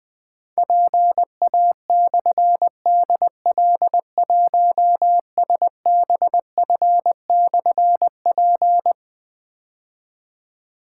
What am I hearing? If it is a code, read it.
Morse code, heard as a keyed tone, about 21 wpm, 700 Hz. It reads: PA/DL1SBF/P
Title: Difficult Call Signs 20wpm